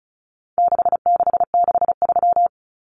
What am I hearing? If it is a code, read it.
Morse code, heard as a keyed tone, about 34 wpm, 700 Hz. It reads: 6663
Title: Numbers - 4-Digit 35wpm